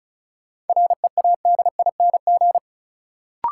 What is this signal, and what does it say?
Morse code, heard as a keyed tone, about 36 wpm, 700 Hz. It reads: READING
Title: Sentences from Top 600 Words - New Word Drill - 35wpm